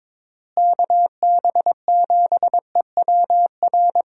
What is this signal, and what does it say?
Morse code, heard as a keyed tone, about 22 wpm, 700 Hz. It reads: KB7EWR
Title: USA Call Signs - 6 Characters - Repeated 2x - 22wpm